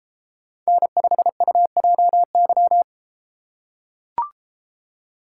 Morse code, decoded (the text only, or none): N5UJY